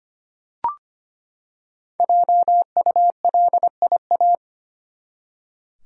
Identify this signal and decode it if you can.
Morse code, heard as a keyed tone, about 25 wpm, 700 Hz. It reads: JULIA
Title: USA Common Names - Max 5 Letters - 25wpm